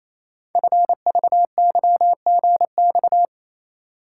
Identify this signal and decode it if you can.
Morse code, heard as a keyed tone, about 27 wpm, 700 Hz. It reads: FVYGX
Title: Code Groups - 5 Letters - 28wpm